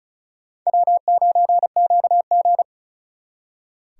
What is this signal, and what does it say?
Morse code, heard as a keyed tone, about 44 wpm, 700 Hz. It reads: W9QG